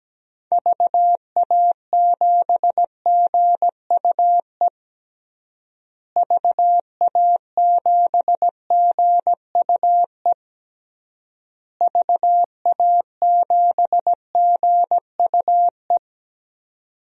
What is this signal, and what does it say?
Morse code, heard as a keyed tone, about 17 wpm, 700 Hz. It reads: VA7GUE VA7GUE VA7GUE
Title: DXCC-100 Call Signs - 6 Characters - Repeated 3x - 17wpm